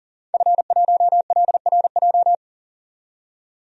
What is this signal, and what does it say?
Morse code, heard as a keyed tone, about 40 wpm, 700 Hz. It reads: F1LRJ
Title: DXCC-100 Call Signs - 5 Characters - 40wpm